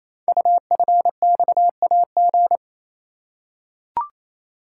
Morse code, heard as a keyed tone, about 27 wpm, 700 Hz. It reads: UFXAG
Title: Code Groups - 5 Letters - 28wpm